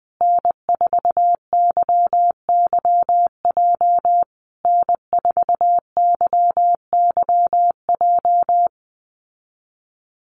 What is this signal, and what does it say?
Morse code, heard as a keyed tone, about 20 wpm, 700 Hz. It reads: N4YYJ N4YYJ
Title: USA Call Signs Repeated 3x 20wpm